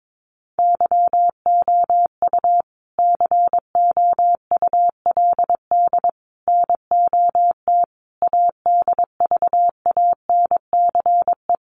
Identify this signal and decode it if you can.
Morse code, heard as a keyed tone, about 22 wpm, 700 Hz. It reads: YOU COULD NOT ADVANCE
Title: Sentences from Top 1400 Words 22wpm